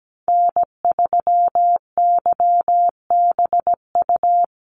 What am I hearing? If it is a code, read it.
Morse code, heard as a keyed tone, about 17 wpm, 700 Hz. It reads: N3YBU
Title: USA Call Signs 17wpm